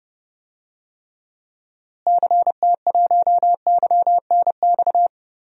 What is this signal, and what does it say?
Morse code, heard as a keyed tone, about 30 wpm, 700 Hz. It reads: CT1YNX
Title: DXCC-100 Call Signs - 6 Characters - Repeated 3x - 30wpm